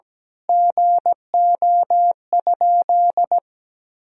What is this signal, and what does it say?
Morse code, heard as a keyed tone, about 17 wpm, 700 Hz. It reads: GO?
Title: Sentences from Top 1000 Words - Review Part 4 - 17wpm